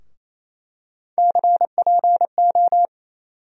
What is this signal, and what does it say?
Morse code, heard as a keyed tone, about 27 wpm, 700 Hz. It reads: CPO
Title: USA Call Sign Postfixes - 3 Letters 28wpm